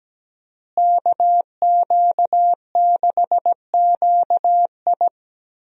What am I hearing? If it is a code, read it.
Morse code, heard as a keyed tone, about 17 wpm, 700 Hz. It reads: KQ6QI
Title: USA Call Signs - 5 Characters - Repeated 2x - 17wpm